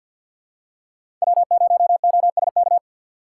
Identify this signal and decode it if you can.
Morse code, heard as a keyed tone, about 51 wpm, 700 Hz. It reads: W0OSK
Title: USA Call Signs - 5 Characters - 3x Character Spacing - 50wpm